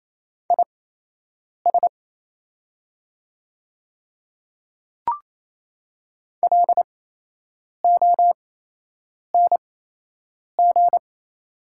Morse code, heard as a keyed tone, about 28 wpm, spaced as for 8 wpm, 700 Hz. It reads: IS LONG
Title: 8x Character Spacing - Top 100 Words - 28wpm